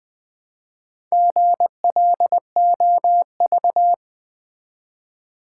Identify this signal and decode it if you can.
Morse code, heard as a keyed tone, about 20 wpm, 700 Hz. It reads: GLOV
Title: Build A Word - 1 Letter At A Time - 5-Letter Words - 20wpm